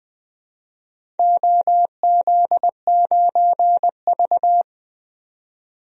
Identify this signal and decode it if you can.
Morse code, heard as a keyed tone, about 20 wpm, 700 Hz. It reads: OZ9V
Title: DXCC-100 Call Signs - 4 Characters - 20wpm